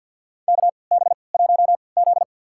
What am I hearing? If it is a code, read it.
Morse code, heard as a keyed tone, about 47 wpm, spaced as for 28 wpm, 700 Hz. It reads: KB1Z
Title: USA Call Signs - 4 Characters - 3x Character Spacing - 50wpm